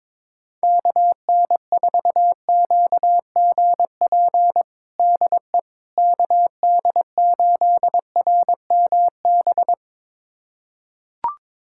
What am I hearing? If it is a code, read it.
Morse code, heard as a keyed tone, about 22 wpm, 700 Hz. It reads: KN4QGP DE KD8RMB
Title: USA Callsign DE Callsign 22wpm